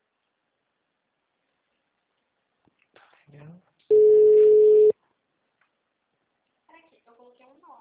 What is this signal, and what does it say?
Ringback tone (Europe, 425 Hz), with speech.